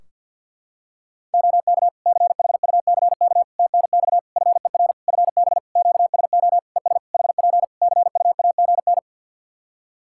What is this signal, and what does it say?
Morse code, heard as a keyed tone, about 51 wpm, 700 Hz. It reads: OK CHUCK TNX FER FB QSO ES HP CUAGN
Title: Practice QSO - Exchange 4 - Ending - 50wpm